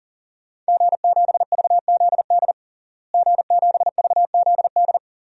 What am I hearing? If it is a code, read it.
Morse code, heard as a keyed tone, about 40 wpm, 700 Hz. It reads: G7VZD G7VZD
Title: DXCC-100 Call Signs - 5 Characters - Repeated 3x - 40wpm